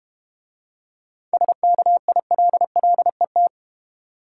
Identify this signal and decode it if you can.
Morse code, heard as a keyed tone, about 31 wpm, 700 Hz. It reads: SKILLET
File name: practice / 25wpm / Difficult Words - Speed-Racing with Hints - 25wpm